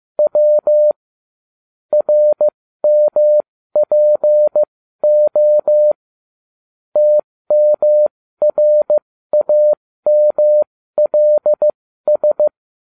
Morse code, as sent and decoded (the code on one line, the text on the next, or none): .-- / .-. -- .--. --- / - -- .-. .- -- .-.. ...
W RMPO TMRAMLS